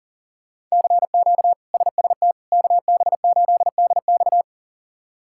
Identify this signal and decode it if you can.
Morse code, heard as a keyed tone, about 37 wpm, 700 Hz. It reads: CQ SST KB8DX
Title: K1USN SST QSO Practice - 40wpm